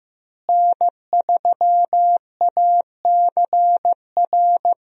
Morse code, heard as a keyed tone, about 15 wpm, 700 Hz. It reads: N3ACR
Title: USA Call Signs - 5 Characters - Repeated 3x - 15wpm